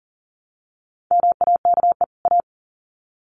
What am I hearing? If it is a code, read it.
Morse code, heard as a keyed tone, about 40 wpm, 700 Hz. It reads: MAKE A
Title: Sets of 2 Words 40wpm